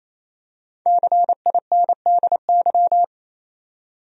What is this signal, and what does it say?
Morse code, heard as a keyed tone, about 28 wpm, 700 Hz. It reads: CINDY
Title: USA Common Names - Max 5 Letters - 28wpm